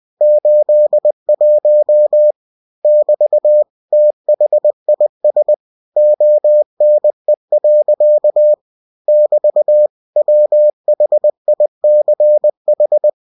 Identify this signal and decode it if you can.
Morse code, heard as a keyed tone, about 20 wpm, 600 Hz. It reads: 81 <BT> THIS ONE. <BT> WHICH
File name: cw-books / winnie-the-pooh / winnie-20wpm-page-0081